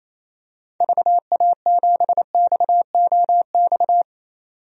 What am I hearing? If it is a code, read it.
Morse code, heard as a keyed tone, about 28 wpm, 700 Hz. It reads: VA7XOX
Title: DXCC-100 Call Signs - 6 Characters - 28wpm